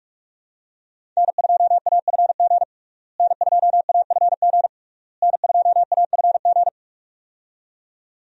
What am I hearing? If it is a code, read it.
Morse code, heard as a keyed tone, about 41 wpm, 700 Hz. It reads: N2AFG N2AFG N2AFG
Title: USA Call Signs - 5 Characters - Repeated 3x - 45wpm